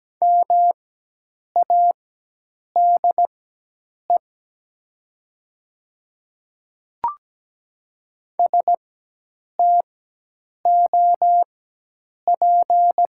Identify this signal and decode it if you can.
Morse code, heard as a keyed tone, about 17 wpm, spaced as for 8 wpm, 700 Hz. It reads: MADE STOP
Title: 4x Character Spacing - Top 300 Words - 17wpm